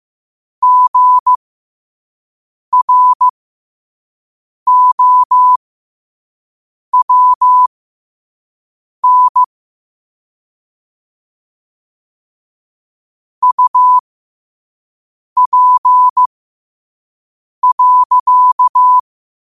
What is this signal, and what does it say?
Morse code, heard as a keyed tone, about 15 wpm, spaced as for 5 wpm, 1000 Hz. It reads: GROWN UP.
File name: qotdfiles / 5wpm